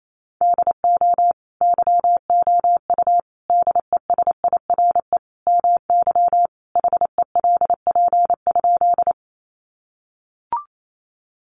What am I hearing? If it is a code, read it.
Morse code, heard as a keyed tone, about 28 wpm, 700 Hz. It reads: DO YOU DESIRE MY HELP?
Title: Sentences from Top 1100 Words 28wpm